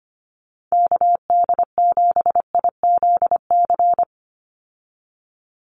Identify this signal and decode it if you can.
Morse code, heard as a keyed tone, about 25 wpm, 690 Hz. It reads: KD7IZC
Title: USA Call Signs 25wpm